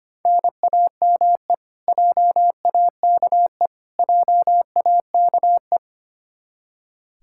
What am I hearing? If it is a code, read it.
Morse code, heard as a keyed tone, about 25 wpm, 700 Hz. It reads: NAME JAKE JAKE
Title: Practice QSO - Exchange 1 - Beginning - 25wpm